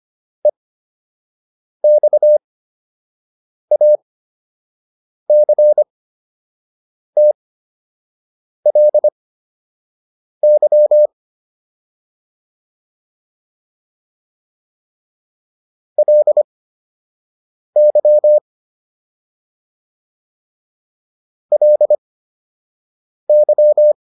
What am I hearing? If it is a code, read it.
Morse code, heard as a keyed tone, about 25 wpm, spaced as for 6 wpm, 600 Hz. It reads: EXACTLY LY LY